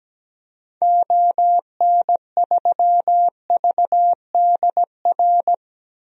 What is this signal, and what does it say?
Morse code, heard as a keyed tone, about 17 wpm, 700 Hz. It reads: ON3VDR
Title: DXCC-100 Call Signs - 6 Characters - Repeated 2x - 17wpm